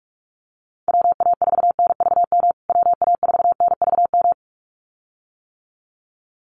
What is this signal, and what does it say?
Morse code, heard as a keyed tone, about 44 wpm, 700 Hz. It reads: WA4NVM WA4NVM
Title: USA Call Signs Repeated 2x 45wpm